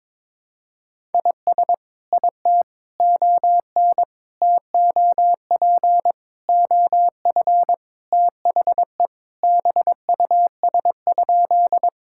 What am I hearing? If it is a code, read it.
Morse code, heard as a keyed tone, about 22 wpm, 700 Hz. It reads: IS IT ON TOP OF THE BUS?
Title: Sentences from Top 1000 Words - Review Part 6 - 22wpm